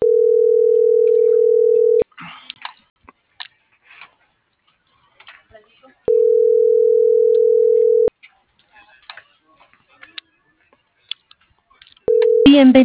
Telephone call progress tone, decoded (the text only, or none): ringback tone